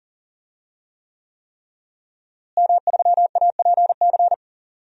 Morse code, heard as a keyed tone, about 40 wpm, 700 Hz. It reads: M3APC
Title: DXCC-100 Call Signs - 5 Characters - Repeated 3x - 40wpm